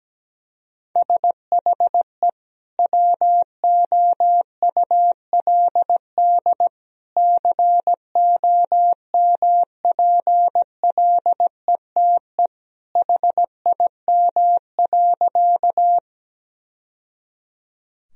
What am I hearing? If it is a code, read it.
Morse code, heard as a keyed tone, about 17 wpm, 700 Hz. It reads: SHE WOULD COMPLETE HIM.
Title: Sentences from Top 1000 Words - Review Part 2 - 17wpm